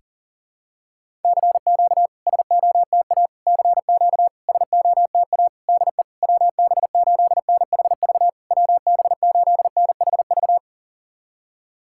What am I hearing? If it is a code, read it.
Morse code, heard as a keyed tone, about 40 wpm, 690 Hz. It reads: CQ SOTA CQ SOTA DE WB8NHV WB8NHV
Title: SOTA QSO Practice 40wpm